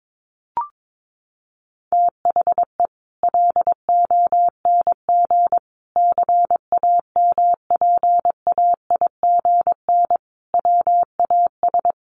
Morse code, heard as a keyed tone, about 22 wpm, 700 Hz. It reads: THE LONG CAMPAIGN WAS
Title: Sentences from Top 1200 Words 22wpm